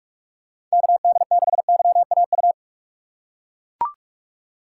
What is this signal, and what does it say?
Morse code, heard as a keyed tone, about 41 wpm, 700 Hz. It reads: KD6YAU
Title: USA Call Signs - 6 Characters - 2x Character Spacing - 45wpm